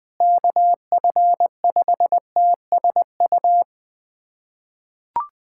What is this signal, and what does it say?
Morse code, heard as a keyed tone, about 20 wpm, 700 Hz. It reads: KF5TSU
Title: USA Callsign DE Callsign - 6 Characters - 20wpm